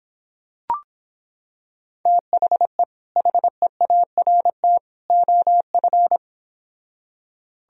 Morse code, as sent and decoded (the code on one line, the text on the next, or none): - .... / .... .- .-. - / --- ..-.
THE HEART OF